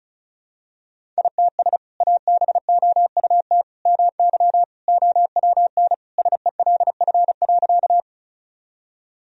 Morse code, heard as a keyed tone, about 36 wpm, 700 Hz. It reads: ITS ABOUT MY OWN SELF.
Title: Sentences from Top 1000 Words - Review Part 1 - 35wpm